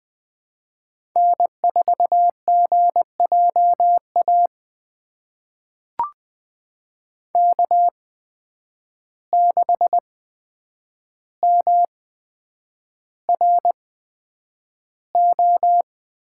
Morse code, heard as a keyed tone, about 20 wpm, spaced as for 9 wpm, 700 Hz. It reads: N4GJA K6MRO
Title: USA Call Signs - 5 Characters - 8x Character Spacing - 20wpm